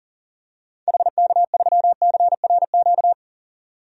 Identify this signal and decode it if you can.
Morse code, heard as a keyed tone, about 40 wpm, 700 Hz. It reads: HK3CRQ
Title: DXCC-100 Call Signs - 6 Characters - 40wpm